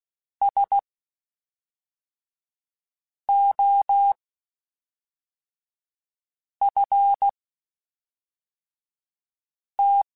Morse code, heard as a keyed tone, about 16 wpm, spaced as for 3 wpm, 800 Hz. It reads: SOFT
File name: Spaced / Chuckles Spaced-mp3s / SoftWords sp @5